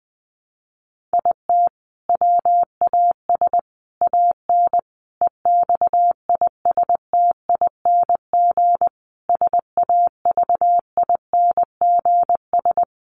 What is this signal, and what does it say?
Morse code, heard as a keyed tone, about 20 wpm, 700 Hz. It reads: IT WAS AN EXISTING SAVINGS
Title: Sentences from Top 1500 Words 20wpm